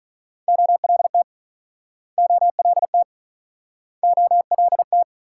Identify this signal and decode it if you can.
Morse code, heard as a keyed tone, about 44 wpm, 690 Hz. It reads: OLT OLT OLT